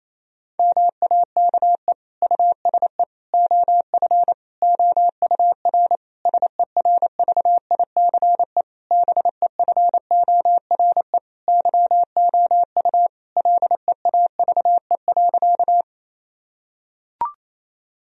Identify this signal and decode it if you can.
Morse code, heard as a keyed tone, about 28 wpm, 700 Hz. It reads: MAKE USE OF OUR SERVICE BEFORE YOU LEAVE.
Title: Sentences from Top 1000 Words - Review Part 4 - 28wpm